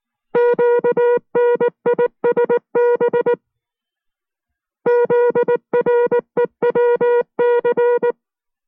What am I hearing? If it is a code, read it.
Morse code, heard as a keyed tone, about 19 wpm, 480 Hz. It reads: QNISB ZREWC